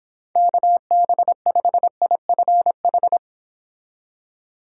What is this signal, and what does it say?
Morse code, heard as a keyed tone, about 31 wpm, 700 Hz. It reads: KB5IFH